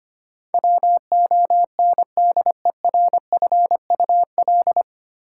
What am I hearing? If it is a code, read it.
Morse code, heard as a keyed tone, about 25 wpm, 700 Hz. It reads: WONDERFUL
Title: Sentences from Top 1100 Words - New Word Drill Alt - 25wpm